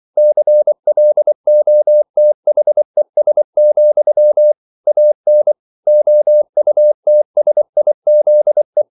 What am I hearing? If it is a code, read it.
Morse code, heard as a keyed tone, about 24 wpm, 600 Hz. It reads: CLOTHES, AN OUTSIZE